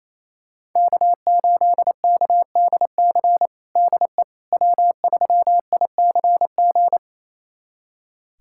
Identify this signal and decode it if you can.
Morse code, heard as a keyed tone, about 28 wpm, 700 Hz. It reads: K8KDC DE W3ICG
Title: USA Callsign DE Callsign - 5 Characters - 28wpm